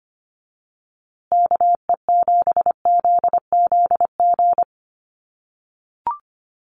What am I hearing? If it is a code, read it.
Morse code, heard as a keyed tone, about 25 wpm, 700 Hz. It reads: KE7ZZG
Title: USA Call Signs Repeated 2x 25wpm